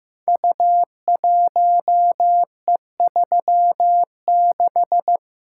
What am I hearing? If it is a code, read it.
Morse code, heard as a keyed tone, about 15 wpm, 700 Hz. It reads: U1E36
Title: Code Groups - 5 Letter:Numbers - 15wpm